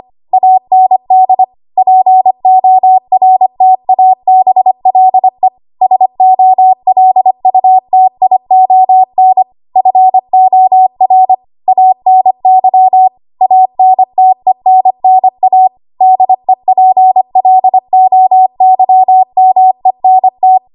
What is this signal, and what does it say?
Morse code, heard as a keyed tone, about 25 wpm, 750 Hz. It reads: AND PORTABLE SOLUTION FOR ANY ANTENNA DEPLOYMENT
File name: Morse / 25WPM